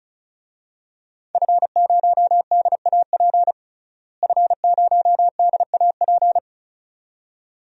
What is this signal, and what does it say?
Morse code, heard as a keyed tone, about 36 wpm, 700 Hz. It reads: F0DAP F0DAP
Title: DXCC-100 Call Signs - 5 Characters - Repeated 2x - 35wpm